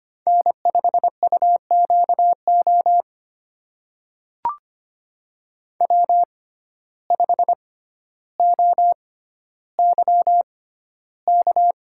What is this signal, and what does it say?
Morse code, heard as a keyed tone, about 25 wpm, spaced as for 13 wpm, 700 Hz. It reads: N5UQO W5OYK
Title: USA Call Signs - 5 Characters - 6x Character Spacing - 25wpm